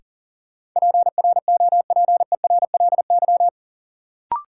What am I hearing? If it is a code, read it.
Morse code, heard as a keyed tone, about 40 wpm, 700 Hz. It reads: PROPERLY
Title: Sentences from Top 1100 Words - New Word Drill Alt - 40wpm